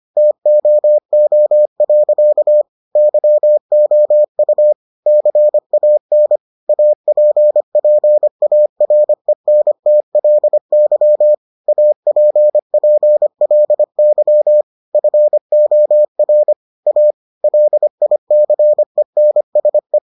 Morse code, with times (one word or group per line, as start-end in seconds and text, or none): 0.2-2.6 TOO.
3.0-4.7 YOU
5.1-6.4 CAN
6.7-11.3 APPARENTLY
11.7-14.6 APPLY
14.9-16.5 FOR
16.9-17.1 A
17.4-20.0 LICENSE